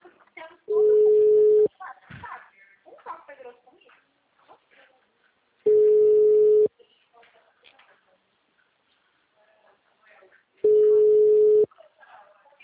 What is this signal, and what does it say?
Ringback tone (Europe, 425 Hz), with speech.
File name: out- -2003- - - .146